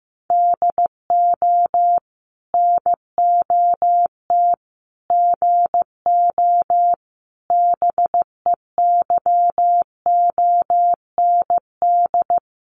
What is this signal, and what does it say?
Morse code, heard as a keyed tone, about 14 wpm, 700 Hz. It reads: DO NOT GO BEYOND
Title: Sentences from Top 1500 Words 15wpm